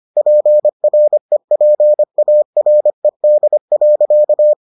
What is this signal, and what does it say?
Morse code, heard as a keyed tone, about 25 wpm, 600 Hz. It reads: PREPARED.